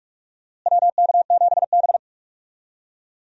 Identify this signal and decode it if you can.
Morse code, heard as a keyed tone, about 42 wpm, 700 Hz. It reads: WK7B
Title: USA Call Signs - 4 Characters - Repeated 3x - 45wpm